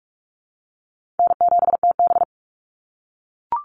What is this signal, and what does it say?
Morse code, heard as a keyed tone, about 44 wpm, 700 Hz. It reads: N7TB
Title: Call Signs - 45wpm